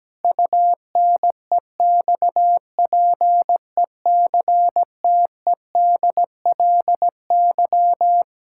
Morse code, heard as a keyed tone, about 17 wpm, 700 Hz. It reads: UNEXPECTEDLY
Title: Pangram Challenge v2 - 17wpm